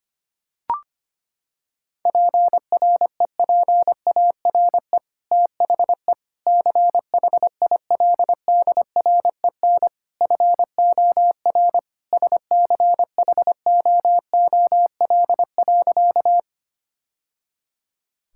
Morse code, heard as a keyed tone, about 25 wpm, 700 Hz. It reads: PREPARE THE CHILDREN FOR SCHOOL.
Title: Sentences from Top 1000 Words - Review Part 6 - 25wpm